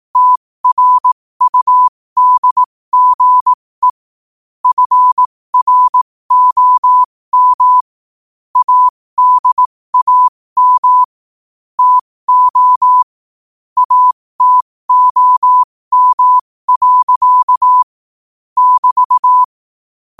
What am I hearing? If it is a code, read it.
Morse code, heard as a keyed tone, about 18 wpm, spaced as for 15 wpm, 1000 Hz. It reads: TRUDGE FROM ADAM TO ATOM. <BT>
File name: qotdfiles / 15wpm